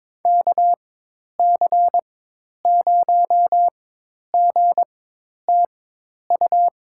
Morse code, heard as a keyed tone, about 22 wpm, spaced as for 10 wpm, 700 Hz. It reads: KC0GTU
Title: USA Call Signs - 6 Characters - 4x Character Spacing - 22wpm